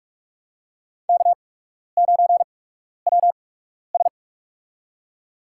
Morse code, heard as a keyed tone, about 42 wpm, spaced as for 12 wpm, 700 Hz. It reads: K9WS
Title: USA Call Signs - 4 Characters - 8x Character Spacing - 45wpm